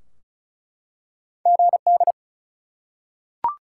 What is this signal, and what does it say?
Morse code, heard as a keyed tone, about 36 wpm, 700 Hz. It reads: GD
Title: Top 180 QSO Elements 35wpm